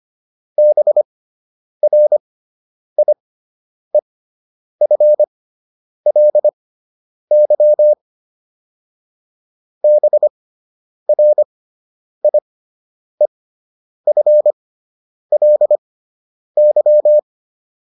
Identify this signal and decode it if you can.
Morse code, heard as a keyed tone, about 25 wpm, spaced as for 9 wpm, 600 Hz. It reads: BRIEFLY BRIEFLY